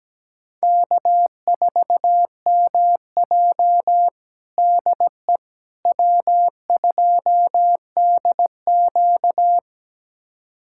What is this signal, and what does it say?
Morse code, heard as a keyed tone, about 17 wpm, 700 Hz. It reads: K4MJ DE W2DQ
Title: USA Callsign DE Callsign - 4 Characters - 17wpm